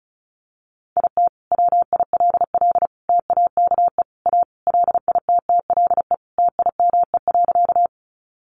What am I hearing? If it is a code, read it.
Morse code, heard as a keyed tone, about 34 wpm, 700 Hz. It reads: IT WILL TAKE A LITTLE TIME.
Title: Sentences from Top 200 Words 35wpm